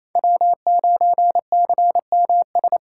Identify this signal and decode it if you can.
Morse code, heard as a keyed tone, about 28 wpm, 700 Hz. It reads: W9CMS
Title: USA Call Signs - 5 Characters - 4x Character Spacing - 28wpm